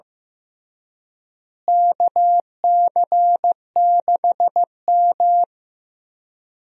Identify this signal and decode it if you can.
Morse code, heard as a keyed tone, about 15 wpm, 700 Hz. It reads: KC6M
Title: USA Call Signs - 4 Characters - Repeated 3x - 15wpm